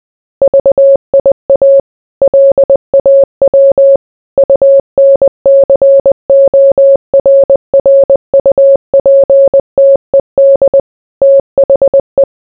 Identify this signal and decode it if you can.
Morse code, heard as a keyed tone, about 20 wpm, 550 Hz. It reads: VIA LAW UNCORRUPTED THE